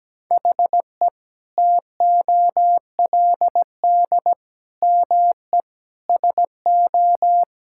Morse code, heard as a keyed tone, about 17 wpm, 700 Hz. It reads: HE TOLD ME SO
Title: Sentences from Top 1600 Words 17wpm